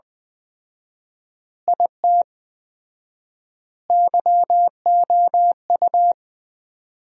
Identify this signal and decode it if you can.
Morse code, heard as a keyed tone, about 20 wpm, 700 Hz. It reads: IT YOU
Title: 4x Word-Spacing - Sentences from Top 200 Words - 20wpm